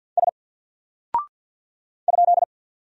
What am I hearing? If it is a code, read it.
Morse code, heard as a keyed tone, about 48 wpm, 700 Hz. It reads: S ?
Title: Single Character (Letter:Number:Punctuation:Prosign) Alt 50wpm